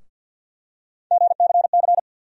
Morse code, heard as a keyed tone, about 50 wpm, 700 Hz. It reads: GCC
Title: USA Call Sign Postfixes - 3 Letters 50wpm